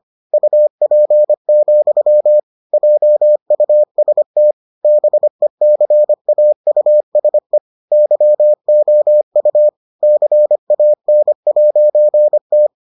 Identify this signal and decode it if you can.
Morse code, heard as a keyed tone, about 25 wpm, 600 Hz. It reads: UP, JUST BECAUSE YOU CAN'T